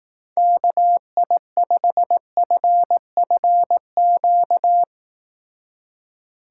Morse code, heard as a keyed tone, about 18 wpm, 700 Hz. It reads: KI5FFQ
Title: Speed Racing - USA 2x3 Call Signs 15wpm